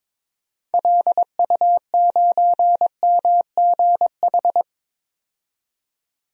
Morse code, heard as a keyed tone, about 22 wpm, 700 Hz. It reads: LU9MGH
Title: DXCC-100 Call Signs - 6 Characters - 22wpm